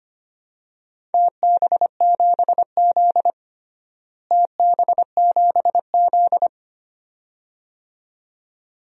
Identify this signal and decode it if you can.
Morse code, heard as a keyed tone, about 25 wpm, 700 Hz. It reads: TB7Z TB7Z
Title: DXCC-100 Call Signs - 4 Characters - Repeated 2x - 25wpm